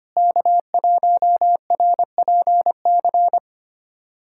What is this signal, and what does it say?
Morse code, heard as a keyed tone, about 25 wpm, 700 Hz. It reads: K1RPC
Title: SOTA QSO Practice 25wpm